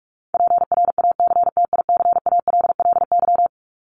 Morse code, heard as a keyed tone, about 44 wpm, 700 Hz. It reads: PRACTICALLY
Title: LY Suffix - Top 100 Words with Base Words 45wpm